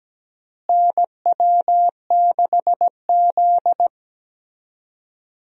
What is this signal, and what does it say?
Morse code, heard as a keyed tone, about 17 wpm, 700 Hz. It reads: NW6Z
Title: USA Call Signs - 4 Characters - Repeated 3x - 17wpm